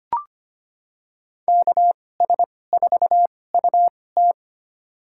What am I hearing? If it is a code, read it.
Morse code, heard as a keyed tone, about 25 wpm, spaced as for 18 wpm, 700 Hz. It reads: KS4UT
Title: USA Call Signs - 2x Character Spacing - 25wpm